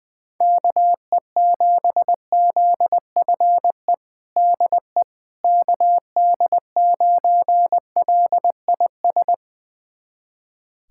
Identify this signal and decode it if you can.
Morse code, heard as a keyed tone, about 20 wpm, 700 Hz. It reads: KE7ZFE DE KD9LIS
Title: USA Callsign DE Callsign 20wpm